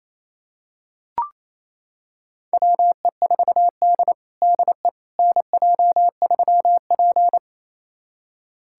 Morse code, heard as a keyed tone, about 28 wpm, 700 Hz. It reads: WE4D DE NJ3P
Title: USA Callsign DE Callsign - 4 Characters - 28wpm